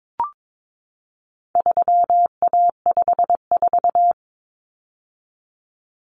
Morse code, heard as a keyed tone, about 22 wpm, 700 Hz. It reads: 3A54
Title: Common Cut Numbers - 4-Digit 22wpm